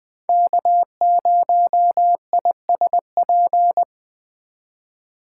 Morse code, heard as a keyed tone, about 21 wpm, 700 Hz. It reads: K0ISP
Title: Speed Racing - USA 1x3 Call Signs 17wpm